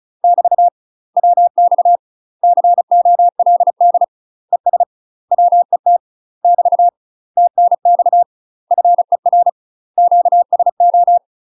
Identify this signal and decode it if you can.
Morse code, heard as a keyed tone, about 34 wpm, 700 Hz. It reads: <BT> WX COLD ES WET <BT> TNX FER QSO
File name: UK MASTER QSOs TRAINING / 35WPM / UK MASTER QSOs 0039